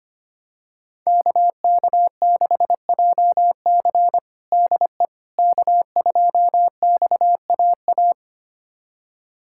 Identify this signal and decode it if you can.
Morse code, heard as a keyed tone, about 25 wpm, 700 Hz. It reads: KK6JC DE K2XAA
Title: USA Callsign DE Callsign - 5 Characters - 25wpm